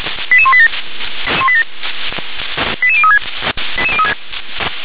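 Five-tone selective call (ZVEI), heard as quick runs of five tones, tones of about 70 ms. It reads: 88187 88036
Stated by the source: ZVEI five-tone selective call received by SDR.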